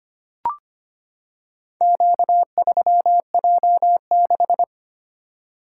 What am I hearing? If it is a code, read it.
Morse code, heard as a keyed tone, about 25 wpm, 700 Hz. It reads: Q3J6
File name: practice / 25wpm / Four Letter-Number Combinations - 25wpm